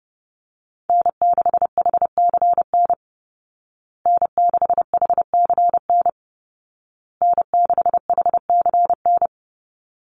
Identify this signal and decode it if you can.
Morse code, heard as a keyed tone, about 30 wpm, 700 Hz. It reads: N6HCN N6HCN N6HCN
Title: Call Signs Repeated 2x - 30wpm